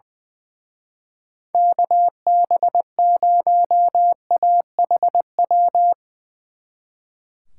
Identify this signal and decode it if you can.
Morse code, heard as a keyed tone, about 20 wpm, 700 Hz. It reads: KB0AHW
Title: Speed Racing - USA Call Signs 20wpm